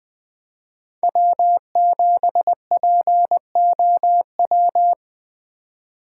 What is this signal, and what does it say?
Morse code, heard as a keyed tone, about 21 wpm, 700 Hz. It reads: W7POW
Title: USA Call Signs - 4x Character Spacing - 20wpm